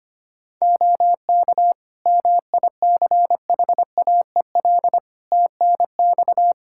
Morse code, heard as a keyed tone, about 25 wpm, 700 Hz. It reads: OK MICHAEL TNX
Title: Practice QSO - Exchange 4 - Ending - 25wpm